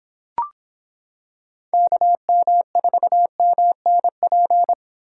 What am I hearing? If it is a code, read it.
Morse code, heard as a keyed tone, about 26 wpm, 700 Hz. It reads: KM4MNP